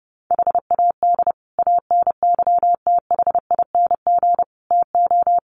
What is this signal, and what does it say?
Morse code, heard as a keyed tone, about 30 wpm, 700 Hz. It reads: HAD ANYTHING TO
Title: Sets of 5 Words Encore 30wpm